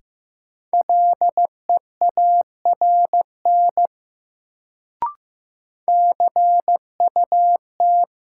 Morse code, heard as a keyed tone, about 15 wpm, 700 Hz. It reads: LEARN CUT
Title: Sentences from Top 300 Words - New Word Drill - 15wpm